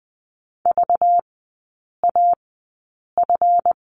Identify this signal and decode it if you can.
Morse code, heard as a keyed tone, about 20 wpm, spaced as for 8 wpm, 700 Hz. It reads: VAF
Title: 191 Lesson 31e - Letter v - Three Characters 20wpm